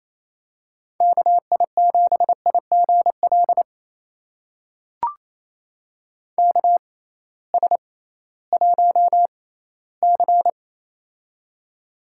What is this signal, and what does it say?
Morse code, heard as a keyed tone, about 28 wpm, spaced as for 16 wpm, 700 Hz. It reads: KI7IGL KS1C
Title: USA Call Signs - 6x Character Spacing - 28wpm